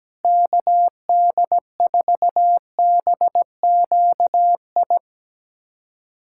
Morse code, heard as a keyed tone, about 17 wpm, 700 Hz. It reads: KD4BQI
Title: USA Callsign DE Callsign - 6 Characters - 17wpm